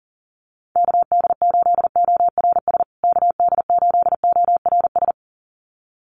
Morse code, heard as a keyed tone, about 40 wpm, 700 Hz. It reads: KD8ORS KD8ORS
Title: USA Call Signs Repeated 2x 40wpm